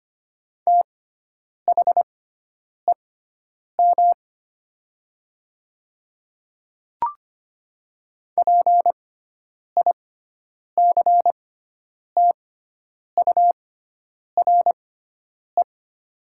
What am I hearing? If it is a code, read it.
Morse code, heard as a keyed tone, about 25 wpm, spaced as for 9 wpm, 700 Hz. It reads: THEM PICTURE
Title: 6x Character Spacing - Top 200 Words - 25wpm